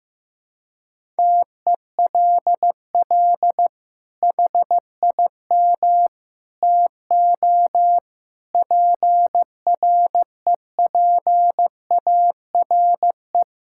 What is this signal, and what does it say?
Morse code, heard as a keyed tone, about 15 wpm, 700 Hz. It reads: TELL HIM TO PREPARE
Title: Sentences from Top 1000 Words - Review Part 6 - 15wpm